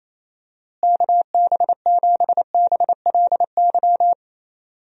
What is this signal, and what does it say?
Morse code, heard as a keyed tone, about 28 wpm, 700 Hz. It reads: KB7BLY
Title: USA Call Signs - 6 Characters - Repeated 3x - 28wpm